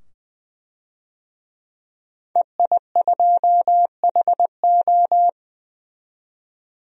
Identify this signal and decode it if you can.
Morse code, heard as a keyed tone, about 20 wpm, 700 Hz. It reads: EI2HO
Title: DXCC-100 Call Signs - 5 Characters - 20wpm